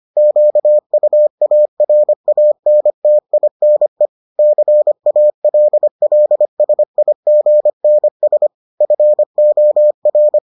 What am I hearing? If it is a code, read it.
Morse code, heard as a keyed tone, about 25 wpm, 600 Hz. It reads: QUARANTINE CALLSIGNS FOR